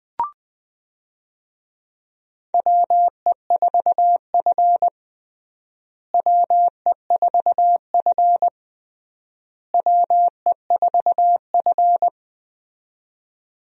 Morse code, heard as a keyed tone, about 20 wpm, 700 Hz. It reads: WE4F WE4F WE4F
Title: USA Call Signs - 4 Characters - Repeated 3x - 20wpm